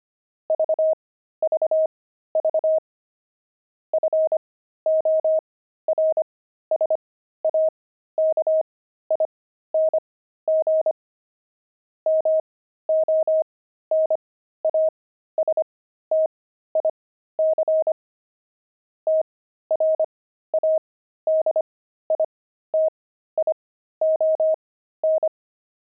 Morse code, heard as a keyed tone, about 25 wpm, spaced as for 13 wpm, 630 Hz. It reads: VVV FORSAKING MONASTIC TRADITION
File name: pangram-fs-13-cs-25-friars